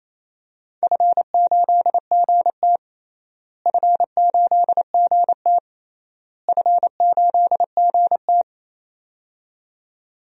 Morse code, heard as a keyed tone, about 28 wpm, 700 Hz. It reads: F8GT F8GT F8GT
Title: DXCC-100 Call Signs - 4 Characters - Repeated 3x - 28wpm